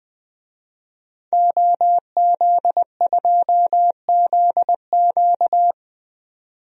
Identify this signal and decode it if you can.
Morse code, heard as a keyed tone, about 20 wpm, 700 Hz. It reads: OZ2ZQ
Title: DXCC-100 Call Signs - 5 Characters - 20wpm